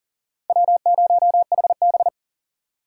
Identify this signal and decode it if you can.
Morse code, heard as a keyed tone, about 40 wpm, 700 Hz. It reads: W0HB
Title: USA Call Signs - 4 Characters - 3x Character Spacing - 40wpm